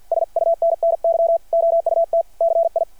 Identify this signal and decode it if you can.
Morse code, heard as a keyed tone, about 52 wpm, 660 Hz. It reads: SUNNY OUT CI
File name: archived / CRSnet